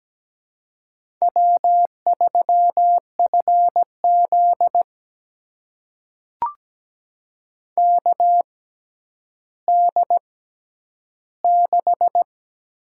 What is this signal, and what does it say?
Morse code, heard as a keyed tone, about 17 wpm, spaced as for 10 wpm, 700 Hz. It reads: W3FZ KD6
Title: USA Call Signs - 6x Character Spacing - 17wpm